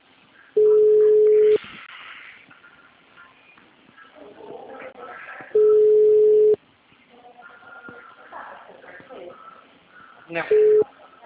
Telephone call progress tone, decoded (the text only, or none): ringback tone